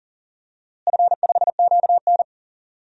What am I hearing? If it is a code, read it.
Morse code, heard as a keyed tone, about 40 wpm, 700 Hz. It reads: F5QN
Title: DXCC-100 Call Signs - 4 Characters - Repeated 3x - 40wpm